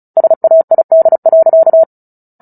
Morse code, heard as a keyed tone, about 34 wpm, 650 Hz. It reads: SAID.
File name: Book Two / 09-Chapter 9 - 35 WPM